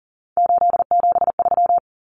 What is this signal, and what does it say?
Morse code, heard as a keyed tone, about 40 wpm, 700 Hz. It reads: 873
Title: Numbers - 3-Digit 40wpm